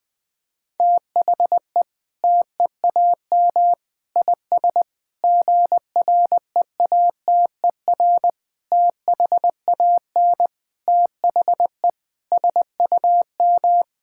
Morse code, heard as a keyed tone, about 20 wpm, 700 Hz. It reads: THE TEAM IS GREATER THAN THE SUM
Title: Sentences with 8-Word Phrases - 20wpm